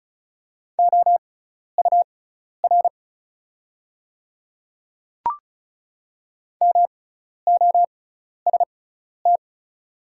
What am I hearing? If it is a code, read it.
Morse code, heard as a keyed tone, about 34 wpm, spaced as for 12 wpm, 700 Hz. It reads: OUR MOST
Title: 6x Character Spacing - Top 200 Words - 35wpm